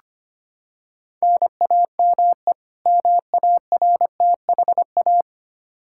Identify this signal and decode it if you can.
Morse code, heard as a keyed tone, about 25 wpm, 700 Hz. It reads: NAME MARTHA
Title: Practice QSO - Exchange 1 - Beginning - 25wpm